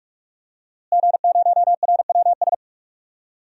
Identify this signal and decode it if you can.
Morse code, heard as a keyed tone, about 44 wpm, 700 Hz. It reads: G0RWS
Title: DXCC-100 Call Signs - 5 Characters - Repeated 2x - 45wpm